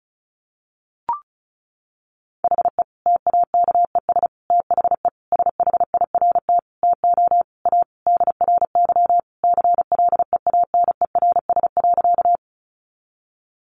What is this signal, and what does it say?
Morse code, heard as a keyed tone, about 34 wpm, 700 Hz. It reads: HE TAKES THE SHIRT TO A DRY CLEANERS.
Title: Additional Easy Phrases 35wpm